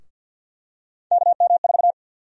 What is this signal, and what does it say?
Morse code, heard as a keyed tone, about 51 wpm, 700 Hz. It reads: KM4
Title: USA Call Sign Prefixes - 50wpm